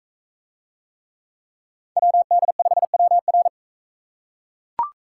Morse code, heard as a keyed tone, about 38 wpm, 700 Hz. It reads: WD5WR